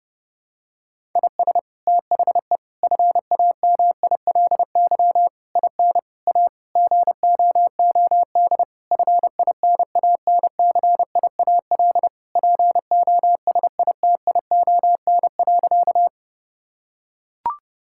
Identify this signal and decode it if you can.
Morse code, heard as a keyed tone, about 30 wpm, 700 Hz. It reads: IS THE FAMILY IN A GOOD FINANCIAL POSITION.
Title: Sentences from Top 1000 Words - Review Part 5 - 30wpm